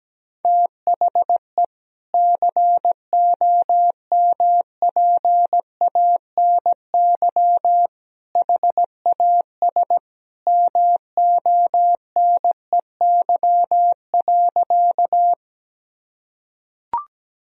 Morse code, heard as a keyed tone, about 17 wpm, 700 Hz. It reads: THE COMPANY HAS MONEY.
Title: Sentences from Top 1000 Words - Review Part 3 - 17wpm